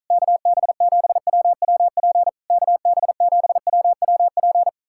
Morse code, heard as a keyed tone, about 47 wpm, 700 Hz. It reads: KB7WWP KB7WWP